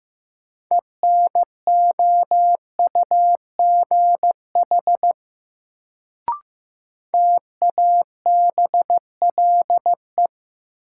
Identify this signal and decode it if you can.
Morse code, heard as a keyed tone, about 15 wpm, 700 Hz. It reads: ENOUGH TABLE
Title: Sentences from Top 400 Words - New Word Drill - 15wpm